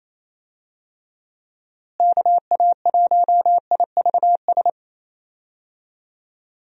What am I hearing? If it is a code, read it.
Morse code, heard as a keyed tone, about 28 wpm, 700 Hz. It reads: KA1IVS
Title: USA Call Signs - 6 Characters - Repeated 2x - 28wpm